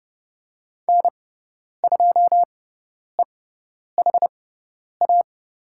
Morse code, heard as a keyed tone, about 29 wpm, spaced as for 10 wpm, 700 Hz. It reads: N2EHA
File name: Lesson 18g - Letter m - Call Signs 30wpm